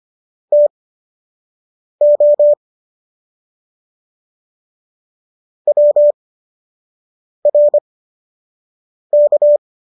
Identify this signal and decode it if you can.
Morse code, heard as a keyed tone, about 25 wpm, spaced as for 6 wpm, 590 Hz. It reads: TO WRK